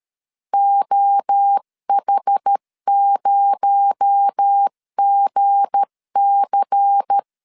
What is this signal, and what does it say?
Morse code, heard as a keyed tone, about 13 wpm, 800 Hz. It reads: OH0GC
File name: LEZIONI / 7 GF9Z / 7 GF9Z 12wpm 2147